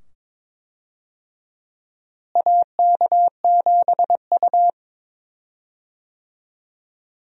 Morse code, heard as a keyed tone, about 22 wpm, 700 Hz. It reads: AK7U
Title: USA Call Signs - 4 Characters - Repeated 2x - 22wpm